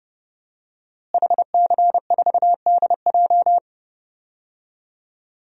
Morse code, heard as a keyed tone, about 30 wpm, 700 Hz. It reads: HC4DJ
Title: DXCC-100 Call Signs - 5 Characters - Repeated 3x - 30wpm